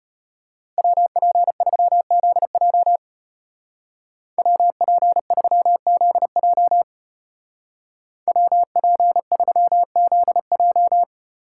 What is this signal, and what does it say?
Morse code, heard as a keyed tone, about 30 wpm, 700 Hz. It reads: WP3ZJ WP3ZJ WP3ZJ
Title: Speed Racing - USA 2x2 Call Signs 25wpm